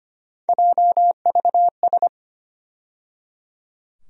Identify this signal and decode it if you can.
Morse code, heard as a keyed tone, about 25 wpm, 700 Hz. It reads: JVS
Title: USA Call Sign Postfixes - 3 Letters 25wpm